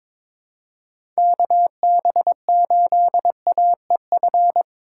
Morse code, heard as a keyed tone, about 22 wpm, 700 Hz. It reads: KB8AEF
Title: USA Call Signs - 3x Character Spacing - 22wpm